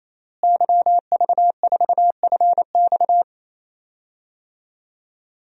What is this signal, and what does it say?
Morse code, heard as a keyed tone, about 27 wpm, 700 Hz. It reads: YV4FX
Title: DXCC-100 Call Signs - 5 Characters - Repeated 3x - 28wpm